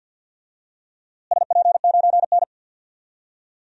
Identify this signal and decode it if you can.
Morse code, heard as a keyed tone, about 52 wpm, 700 Hz. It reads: SP9N
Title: DXCC-100 Call Signs - 4 Characters - Repeated 2x - 50wpm